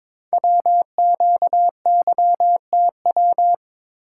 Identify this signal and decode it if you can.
Morse code, heard as a keyed tone, about 22 wpm, 700 Hz. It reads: WQYTW
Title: Code Groups - 5 Letters - 22wpm